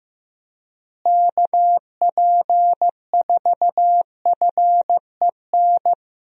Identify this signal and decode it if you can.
Morse code, heard as a keyed tone, about 15 wpm, 700 Hz. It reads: KP4FEN
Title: DXCC-100 Call Signs - 6 Characters - Repeated 2x - 15wpm